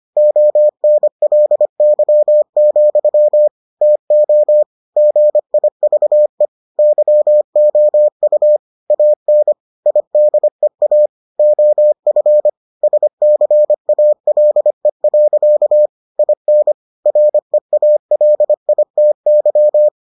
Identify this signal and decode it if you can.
Morse code, heard as a keyed tone, about 25 wpm, 600 Hz. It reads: ONLY, TO GIVE YOU AN IDEA OF SCALE. IN REALITY